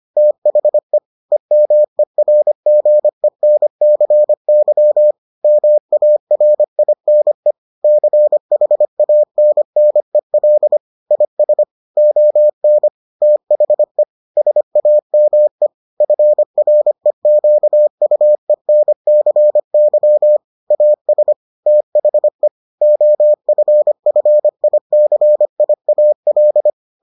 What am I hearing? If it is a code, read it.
Morse code, heard as a keyed tone, about 25 wpm, 600 Hz. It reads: THE EMERGENCY MARINE CHANNEL IS ON THE SAME FREQUENCY AS THE OFFICIAL